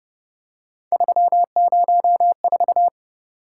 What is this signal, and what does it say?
Morse code, heard as a keyed tone, about 31 wpm, 700 Hz. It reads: 304